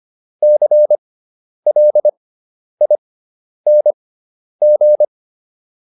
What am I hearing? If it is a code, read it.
Morse code, heard as a keyed tone, about 25 wpm, spaced as for 10 wpm, 600 Hz. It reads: CLING
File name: ING5 10